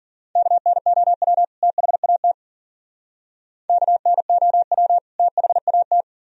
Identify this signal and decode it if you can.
Morse code, heard as a keyed tone, about 48 wpm, 700 Hz. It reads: KNOW THAT KNOW THAT
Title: Speed-Racing - Sets of 2 Words 40wpm